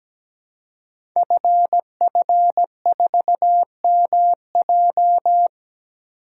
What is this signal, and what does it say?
Morse code, heard as a keyed tone, about 17 wpm, 700 Hz. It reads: FF4MJ
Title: Code Groups - 5 Letter:Numbers - 17wpm